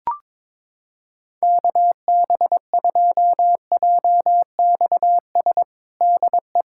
Morse code, heard as a keyed tone, about 21 wpm, 700 Hz. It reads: KB2JXS DE
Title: USA Callsign DE Callsign 22wpm